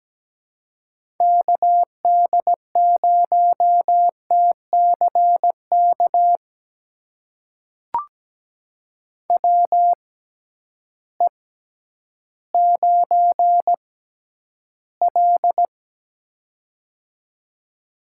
Morse code, heard as a keyed tone, about 17 wpm, spaced as for 10 wpm, 700 Hz. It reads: KD0TCK WE9L
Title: USA Call Signs - 6x Character Spacing - 17wpm